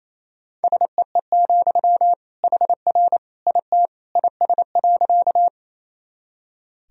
Morse code, heard as a keyed tone, about 28 wpm, 700 Hz. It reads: SEE, HR IT IS.
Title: Sentences from Top 200 Words and Abbreviations 28wpm